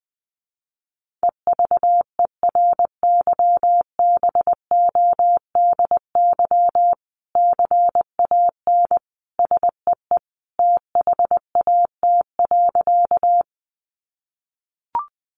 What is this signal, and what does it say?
Morse code, heard as a keyed tone, about 20 wpm, 700 Hz. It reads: EVERYBODY CAN SEE THAT.
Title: Sentences from Top 1400 Words 20wpm